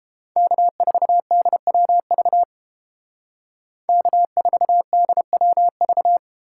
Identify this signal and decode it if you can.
Morse code, heard as a keyed tone, about 30 wpm, 700 Hz. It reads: K4DWV K4DWV